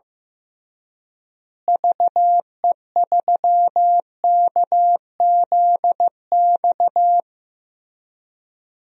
Morse code, heard as a keyed tone, about 15 wpm, 700 Hz. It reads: VE3KZX
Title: DXCC-100 Call Signs - 6 Characters - Repeated 3x - 15wpm